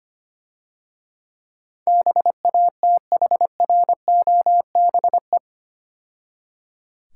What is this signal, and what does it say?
Morse code, heard as a keyed tone, about 27 wpm, 690 Hz. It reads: BATHROBE
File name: practice / 25wpm / Difficult Words - Speed-Racing with Hints - 25wpm